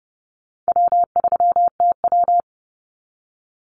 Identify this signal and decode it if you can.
Morse code, heard as a keyed tone, about 30 wpm, 700 Hz. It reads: W3TW
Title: Call Signs - 30wpm